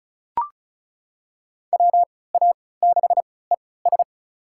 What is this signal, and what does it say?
Morse code, heard as a keyed tone, about 34 wpm, spaced as for 20 wpm, 700 Hz. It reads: WA6ES
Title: USA Call Signs - 3x Character Spacing - 35wpm